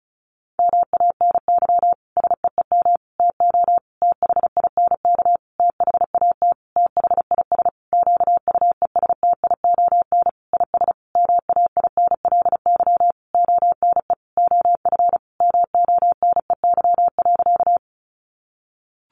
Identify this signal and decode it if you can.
Morse code, heard as a keyed tone, about 34 wpm, 700 Hz. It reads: MANY SEEM TO THINK THAT THIS QUESTION IS MAINLY ONE OF MONEY.
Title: Sentences from Top 1400 Words 35wpm